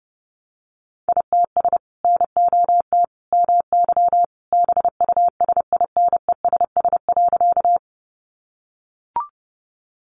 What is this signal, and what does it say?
Morse code, heard as a keyed tone, about 30 wpm, 700 Hz. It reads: ITS NOT MY BUSINESS.
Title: Sentences from Top 600 Words 30wpm